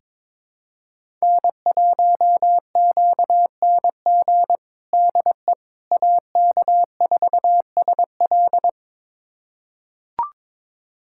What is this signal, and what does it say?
Morse code, heard as a keyed tone, about 22 wpm, 700 Hz. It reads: N1QNG DE AK4SL
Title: USA Callsign DE Callsign - 5 Characters - 22wpm